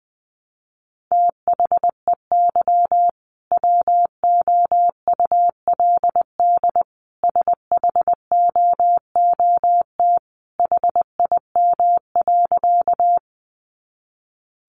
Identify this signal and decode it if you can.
Morse code, heard as a keyed tone, about 20 wpm, 700 Hz. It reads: THEY WOULD SHOOT HIM.
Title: Sentences from Top 1200 Words 20wpm